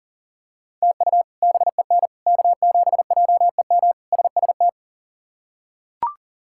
Morse code, heard as a keyed tone, about 40 wpm, 700 Hz. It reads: TU BEN K7JEM SST
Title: K1USN SST QSO Practice - 40wpm